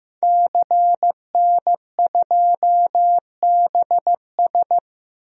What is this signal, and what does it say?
Morse code, heard as a keyed tone, about 16 wpm, 700 Hz. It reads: CN2BS
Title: DXCC-100 Call Signs - 5 Characters - Repeated 2x - 15wpm